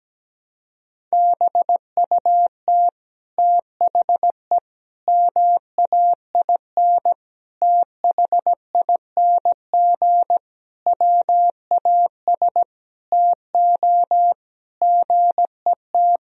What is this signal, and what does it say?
Morse code, heard as a keyed tone, about 17 wpm, 700 Hz. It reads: BUT THE MAIN THING WAS TO GET
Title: Sentences from Top 1000 Words - Review Part 2 - 17wpm